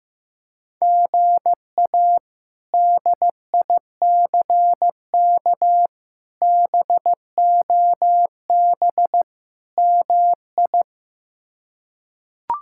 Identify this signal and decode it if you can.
Morse code, heard as a keyed tone, about 15 wpm, 700 Hz. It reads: GA DICK BOB MI
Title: K1USN SST QSO Practice - 15wpm